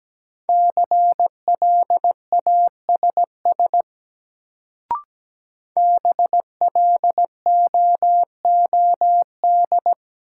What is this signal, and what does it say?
Morse code, heard as a keyed tone, about 17 wpm, 700 Hz. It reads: CLASS BLOOD
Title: Sentences from Top 1000 Words - Review Part 2 - New Word Drill - 17wpm